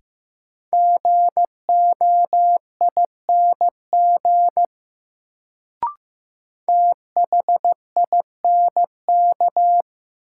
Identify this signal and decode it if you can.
Morse code, heard as a keyed tone, about 15 wpm, 700 Hz. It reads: GOING THINK
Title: Sets of 2 Words - New Word Drill - 15wpm